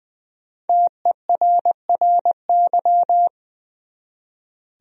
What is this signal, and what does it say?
Morse code, heard as a keyed tone, about 20 wpm, 700 Hz. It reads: TERRY
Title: USA Common Names - Max 5 Letters - 20wpm